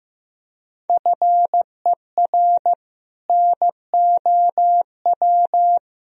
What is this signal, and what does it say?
Morse code, heard as a keyed tone, about 15 wpm, 700 Hz. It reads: FER NOW
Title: Sentences from Top 200 Words and Abbreviations 15wpm